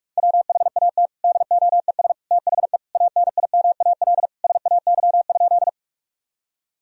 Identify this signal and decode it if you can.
Morse code, heard as a keyed tone, about 46 wpm, 700 Hz. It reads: WHAT DOES THE ANIMAL SAY?
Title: Sentences from Top 1000 Words - Review Part 1 - 45wpm